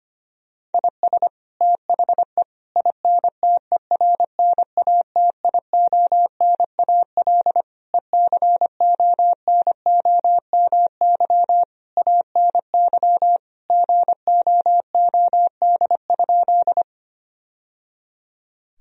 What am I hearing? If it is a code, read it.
Morse code, heard as a keyed tone, about 25 wpm, 700 Hz. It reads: IS THE INTERNATIONAL ECONOMY ANY GOOD?
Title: Sentences from Top 1000 Words - Review Part 4 - 25wpm